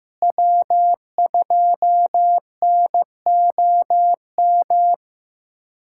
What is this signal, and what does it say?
Morse code, heard as a keyed tone, about 16 wpm, 700 Hz. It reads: W2NOM
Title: USA Callsign DE Callsign - 5 Characters - 15wpm